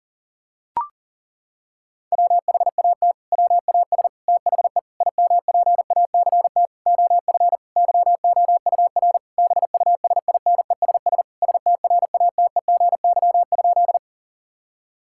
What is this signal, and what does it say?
Morse code, heard as a keyed tone, about 37 wpm, 700 Hz. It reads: WHAT WAS THE IMPACT OF YOUR BUSINESS STRATEGY?
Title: Sentences from Top 1000 Words - Review Part 6 - 40wpm